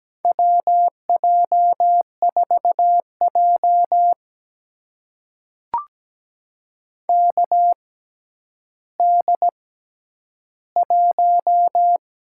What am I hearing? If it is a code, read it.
Morse code, heard as a keyed tone, about 17 wpm, spaced as for 10 wpm, 700 Hz. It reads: WJ4J KD1
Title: USA Call Signs - 4 Characters - 6x Character Spacing - 17wpm